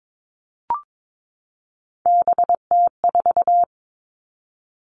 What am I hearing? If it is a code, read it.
Morse code, heard as a keyed tone, about 22 wpm, 700 Hz. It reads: BT4
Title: All Cut Numbers - 3-Digit 22wpm